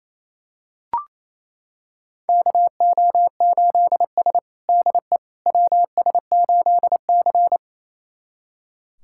Morse code, heard as a keyed tone, about 28 wpm, 700 Hz. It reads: KO8S DE WS8C
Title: USA Callsign DE Callsign - 4 Characters - 28wpm